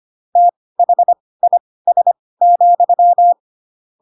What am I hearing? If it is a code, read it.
Morse code, heard as a keyed tone, about 25 wpm, spaced as for 18 wpm, 700 Hz. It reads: THIS,
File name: Pinocchio ch3 12 25 ik1hge